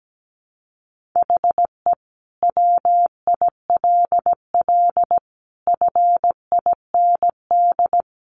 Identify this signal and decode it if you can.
Morse code, heard as a keyed tone, about 17 wpm, 700 Hz. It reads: HE WILL FIND
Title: Sentences from Top 1400 Words 17wpm